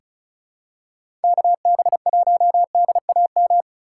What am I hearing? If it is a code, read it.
Morse code, heard as a keyed tone, about 34 wpm, 700 Hz. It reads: KB1DAM
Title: USA Call Signs - 6 Characters - 2x Character Spacing - 35wpm